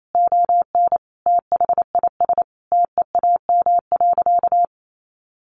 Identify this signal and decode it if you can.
Morse code, heard as a keyed tone, about 28 wpm, 700 Hz. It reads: ON THIS TEAM.
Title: Sentences from Top 1500 Words 28wpm